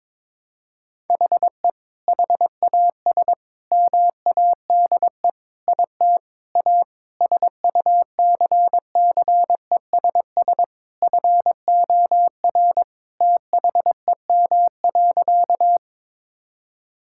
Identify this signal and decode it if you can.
Morse code, heard as a keyed tone, about 22 wpm, 700 Hz. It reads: HE HAS MADE IT A SUCCESS FOR THEM.
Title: Sentences from Top 1000 Words - Review Part 6 - 22wpm